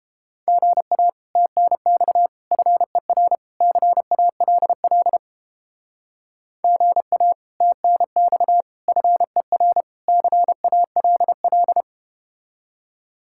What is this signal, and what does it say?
Morse code, heard as a keyed tone, about 31 wpm, 700 Hz. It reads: GA TNX FER CALL GA TNX FER CALL